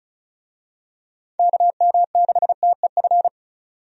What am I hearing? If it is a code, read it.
Morse code, heard as a keyed tone, about 36 wpm, 700 Hz. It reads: KM6TEF
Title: USA Call Signs - 6 Characters - 35wpm